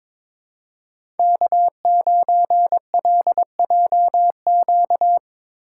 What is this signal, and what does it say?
Morse code, heard as a keyed tone, about 22 wpm, 700 Hz. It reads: K9LJQ
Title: USA Call Signs - 5 Characters - 4x Character Spacing - 22wpm